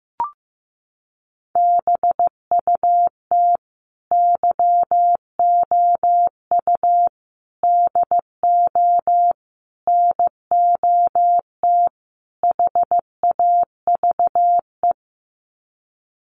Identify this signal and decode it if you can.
Morse code, heard as a keyed tone, about 15 wpm, 700 Hz. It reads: BUT YOU DO NOT HAVE
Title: Sets of 5 Words Encore 15wpm